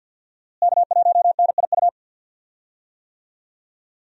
Morse code, heard as a keyed tone, about 49 wpm, 700 Hz. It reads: K1NIU
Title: USA Call Signs - 5 Characters - Repeated 2x - 50wpm